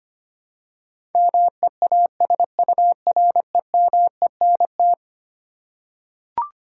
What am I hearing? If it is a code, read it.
Morse code, heard as a keyed tone, about 25 wpm, 700 Hz. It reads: MEASUREMENT
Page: morse-code-ninja-practice-25wpm